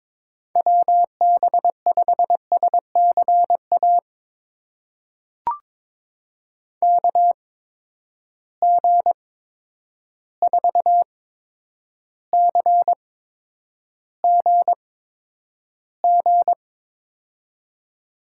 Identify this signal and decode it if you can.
Morse code, heard as a keyed tone, about 22 wpm, spaced as for 9 wpm, 700 Hz. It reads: WB5SCA KG4CGG
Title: USA Call Signs - 6 Characters - 8x Character Spacing - 22wpm